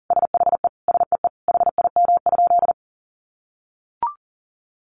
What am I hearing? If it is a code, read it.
Morse code, heard as a keyed tone, about 40 wpm, 700 Hz. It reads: SHE SEE HIM?
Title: Sentences from Top 100 Words 40wpm